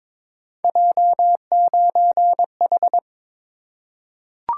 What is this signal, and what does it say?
Morse code, heard as a keyed tone, about 22 wpm, 700 Hz. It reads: J9H
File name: practice / 22wpm / Three Letter-Number Combinations - 22wpm